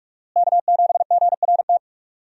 Morse code, heard as a keyed tone, about 43 wpm, 700 Hz. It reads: K7GRT
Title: USA Call Signs - 5 Characters - Repeated 3x - 45wpm